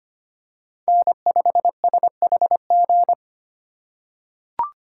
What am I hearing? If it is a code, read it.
Morse code, heard as a keyed tone, about 25 wpm, 700 Hz. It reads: N5SHG
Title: USA Call Signs - 5 Characters - 4x Character Spacing - 25wpm